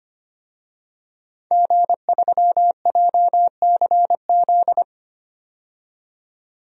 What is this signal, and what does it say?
Morse code, heard as a keyed tone, about 25 wpm, 700 Hz. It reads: G3JCZ
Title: DXCC-100 Call Signs - 5 Characters - 25wpm